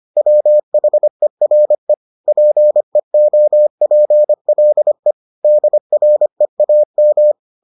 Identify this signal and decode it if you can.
Morse code, heard as a keyed tone, about 25 wpm, 600 Hz. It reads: WHERE PEOPLE DREAM